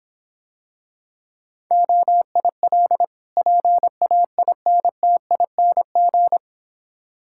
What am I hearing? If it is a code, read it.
Morse code, heard as a keyed tone, about 26 wpm, 700 Hz. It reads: OIL PAINTING